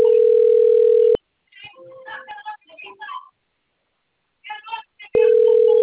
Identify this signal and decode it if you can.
Ringback tone (North America), 440 + 480 Hz.